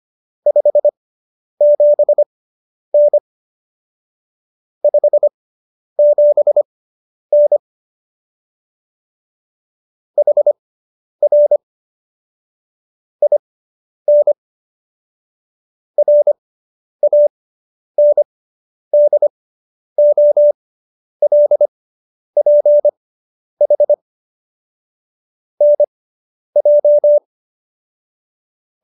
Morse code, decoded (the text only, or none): 57N 57N HR IN RANDOLPH NJ